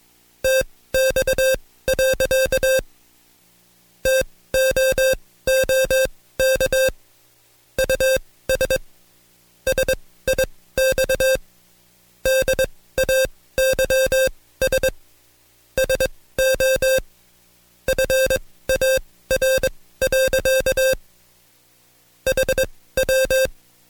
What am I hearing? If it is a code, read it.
Morse code, heard as a keyed tone, about 21 wpm, spaced as for 16 wpm, 530 Hz. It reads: TX. TOOK US SIX DAYS SO FAR. HW
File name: Code Quick / AudioDownld2 / Seg134-Stepper7 15WPM